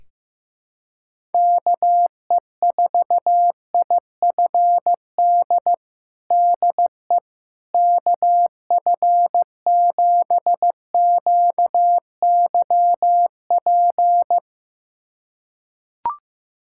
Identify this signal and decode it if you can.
Morse code, heard as a keyed tone, about 15 wpm, 700 Hz. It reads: KE4IFD DE KF7QYP
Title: USA Callsign DE Callsign - 6 Characters - 15wpm